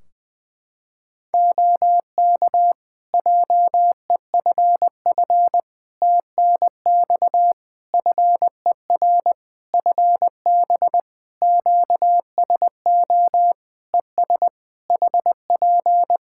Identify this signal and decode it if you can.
Morse code, heard as a keyed tone, about 20 wpm, 700 Hz. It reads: OK JEFF TNX FER FB QSO ES HP
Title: Practice QSO - Exchange 4 - Ending - 20wpm